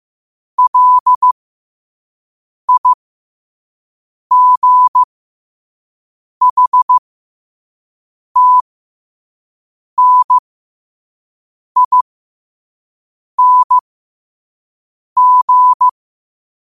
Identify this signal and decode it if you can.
Morse code, heard as a keyed tone, about 15 wpm, spaced as for 5 wpm, 1000 Hz. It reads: LIGHTNING
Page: qotd-at-5-wpm